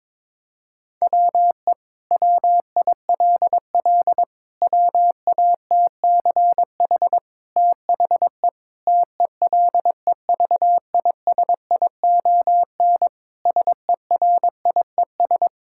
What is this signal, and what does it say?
Morse code, heard as a keyed tone, about 22 wpm, 700 Hz. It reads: WE WILL WATCH THE TELEVISION SERIES
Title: Sentences from Top 1000 Words - Review Part 5 - 22wpm